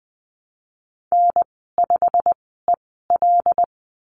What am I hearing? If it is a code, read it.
Morse code, heard as a keyed tone, about 20 wpm, spaced as for 15 wpm, 700 Hz. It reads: N5EL
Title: 088 Lesson 15g - Number 5 - Call Signs 20wpm